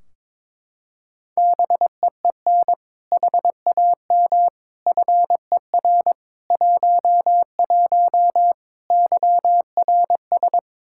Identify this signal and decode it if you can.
Morse code, heard as a keyed tone, about 22 wpm, 700 Hz. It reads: BEEN HAM FER 11 YRS
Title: Practice QSO - Exchange 3 - More Info - 22wpm